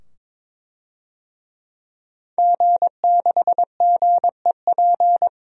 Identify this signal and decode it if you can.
Morse code, heard as a keyed tone, about 22 wpm, 700 Hz. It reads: G6GEP
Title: DXCC-100 Call Signs - 5 Characters - 22wpm